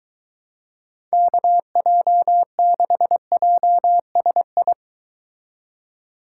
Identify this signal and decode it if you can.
Morse code, heard as a keyed tone, about 23 wpm, 700 Hz. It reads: KJ6JSI